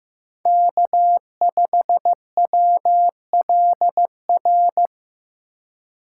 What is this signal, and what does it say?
Morse code, heard as a keyed tone, about 15 wpm, 700 Hz. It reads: K5WLR
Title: USA Call Signs - 5 Characters - 4x Character Spacing - 15wpm